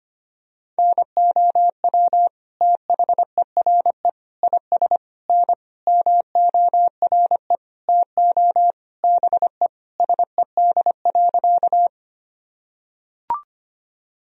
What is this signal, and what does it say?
Morse code, heard as a keyed tone, about 25 wpm, 700 Hz. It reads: NOW THERE IS N MORE TO BE SED.
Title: Sentences from Top 100 Words and Abbreviations 25wpm